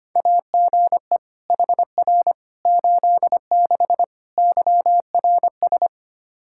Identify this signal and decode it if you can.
Morse code, heard as a keyed tone, about 25 wpm, 700 Hz. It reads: AGE HR 86 YRS
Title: Practice QSO - Exchange 3 - More Info - 25wpm